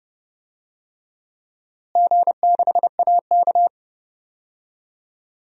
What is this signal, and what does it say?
Morse code, heard as a keyed tone, about 30 wpm, 700 Hz. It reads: G6AK
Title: DXCC-100 Call Signs - 4 Characters - Repeated 2x - 30wpm